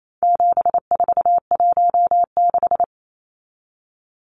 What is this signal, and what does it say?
Morse code, heard as a keyed tone, about 28 wpm, 700 Hz. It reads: 7416
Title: Numbers - 4-Digit 28wpm